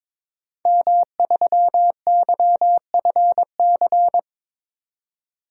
Morse code, heard as a keyed tone, about 22 wpm, 700 Hz. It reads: M3YFC
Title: DXCC-100 Call Signs - 5 Characters - 22wpm